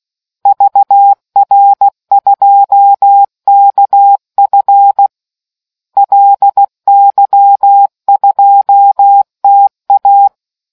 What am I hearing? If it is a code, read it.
Morse code, heard as a keyed tone, about 16 wpm, 800 Hz. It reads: VR2KF LY2TA